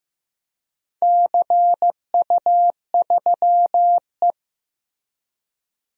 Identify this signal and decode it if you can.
Morse code, heard as a keyed tone, about 16 wpm, 700 Hz. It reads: CU3E
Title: DXCC-100 Call Signs - 4 Characters - Repeated 3x - 15wpm